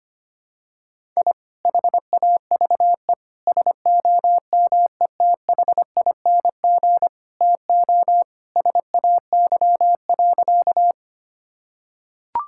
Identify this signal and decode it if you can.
Morse code, heard as a keyed tone, about 25 wpm, 700 Hz. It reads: I HAVE SOMETHING TO SAY.
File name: practice / 25wpm / Sentences with 2-Word Phrases - 25wpm